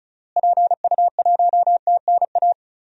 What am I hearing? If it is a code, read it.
Morse code, heard as a keyed tone, about 34 wpm, 700 Hz. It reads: PU1TNA
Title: DXCC-100 Call Signs - 6 Characters - Repeated 2x - 35wpm